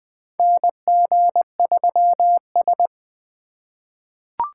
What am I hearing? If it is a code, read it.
Morse code, heard as a keyed tone, about 20 wpm, 700 Hz. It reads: NG3S
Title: USA Call Signs - 2x Character Spacing - 20wpm